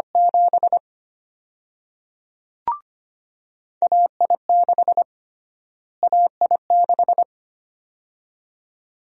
Morse code, heard as a keyed tone, about 25 wpm, 700 Hz. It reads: 7 AI6 AI6
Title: USA Call Sign Prefixes - Repeated 2x - 25wpm